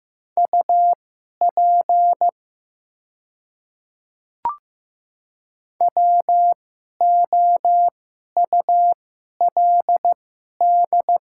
Morse code, heard as a keyed tone, about 15 wpm, spaced as for 11 wpm, 700 Hz. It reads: UP WOULD
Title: 2x Character Spacing - Top 300 Words - 15wpm